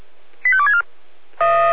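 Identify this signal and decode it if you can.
Five-tone selective call (ZVEI), heard as quick runs of five tones, tones of about 70 ms. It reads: 85264
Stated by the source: ZVEI five-tone selective call received by SDR.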